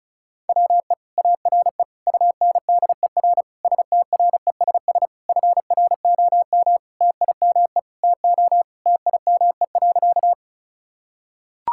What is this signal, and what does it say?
Morse code, heard as a keyed tone, about 36 wpm, 700 Hz. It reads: WE ARE UNDER STRESS FROM TIME TO TIME.
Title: Sentences from Top 1000 Words - Review Part 6 - 35wpm